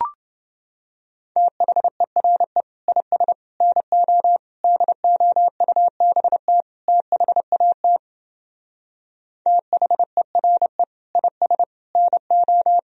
Morse code, heard as a keyed tone, about 29 wpm, 700 Hz. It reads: THERE IS NO DOUBT THAT THERE IS NO